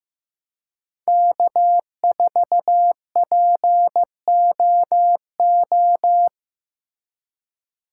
Morse code, heard as a keyed tone, about 15 wpm, 700 Hz. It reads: K4POO
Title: USA Call Signs - 5 Characters - 4x Character Spacing - 15wpm